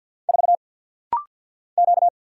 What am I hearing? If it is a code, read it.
Morse code, heard as a keyed tone, about 50 wpm, 700 Hz. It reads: 4 Q
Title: Single Letter-Number Alt - Rapid-Fire 50wpm